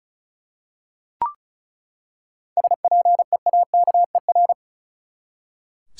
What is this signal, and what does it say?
Morse code, heard as a keyed tone, about 34 wpm, 700 Hz. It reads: SPEAKER
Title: Seven Letter Words - 35wpm